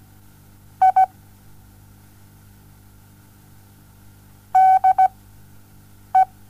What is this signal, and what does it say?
Morse code, heard as a keyed tone, about 16 wpm, spaced as for 7 wpm, 760 Hz. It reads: I DE
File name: Onlinelehrgang / morsen / DJ4UF Morsen07